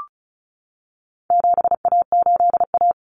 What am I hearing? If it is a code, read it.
Morse code, heard as a keyed tone, about 34 wpm, 700 Hz. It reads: 7A8A
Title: Common Cut Numbers - 4-Digit 35wpm